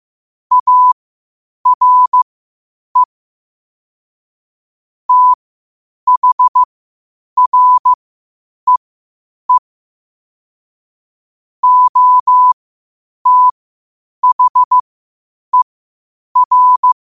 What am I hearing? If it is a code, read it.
Morse code, heard as a keyed tone, about 15 wpm, spaced as for 8 wpm, 1000 Hz. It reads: ARE THREE OTHER
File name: qotdfiles / 8wpm